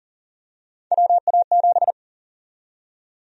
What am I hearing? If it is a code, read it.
Morse code, heard as a keyed tone, about 40 wpm, 700 Hz. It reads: WA7
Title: USA Call Sign Prefixes - 40wpm